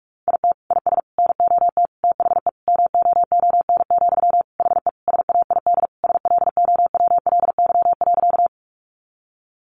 Morse code, heard as a keyed tone, about 44 wpm, 700 Hz. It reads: IT IS NOT THE MOON, HE SAID SLOWLY.
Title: Sentences from Top 1100 Words 45wpm